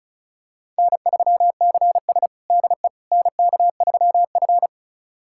Morse code, heard as a keyed tone, about 36 wpm, 700 Hz. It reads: N3CS DE NK3F
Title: USA Callsign DE Callsign - 4 Characters - 35wpm